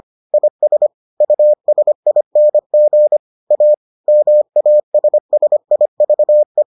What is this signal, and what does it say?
Morse code, heard as a keyed tone, about 25 wpm, 600 Hz. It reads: IS USING A MASSIVE